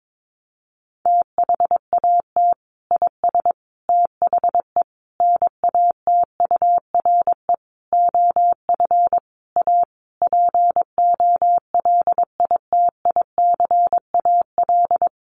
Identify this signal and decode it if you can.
Morse code, heard as a keyed tone, about 22 wpm, 700 Hz. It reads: THAT IS THE NATURE OF A POLITICAL
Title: Sentences from Top 1400 Words 22wpm